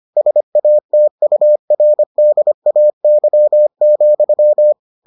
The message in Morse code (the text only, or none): SATURDAY,